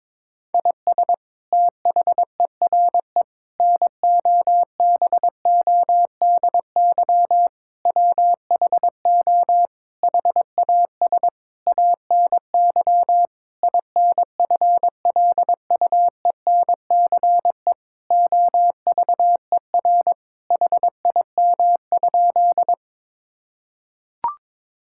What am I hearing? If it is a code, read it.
Morse code, heard as a keyed tone, about 22 wpm, 700 Hz. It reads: IS THERE NOBODY WHO HAS ANY INFLUENCE OVER HIM?
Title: Sentences from Top 1600 Words 22wpm